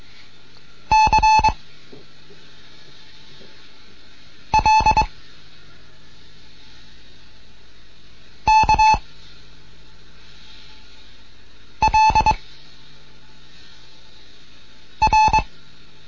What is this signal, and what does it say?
Morse code, heard as a keyed tone, about 22 wpm, 870 Hz. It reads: C L K L R